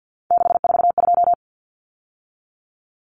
Morse code, heard as a keyed tone, about 50 wpm, 700 Hz. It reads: 642
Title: Numbers - 3-Digit 50wpm